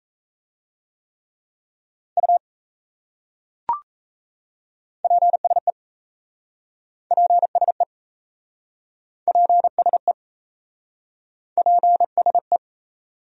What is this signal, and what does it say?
Morse code, heard as a keyed tone, about 37 wpm, 700 Hz. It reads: U PSE PSE PSE PSE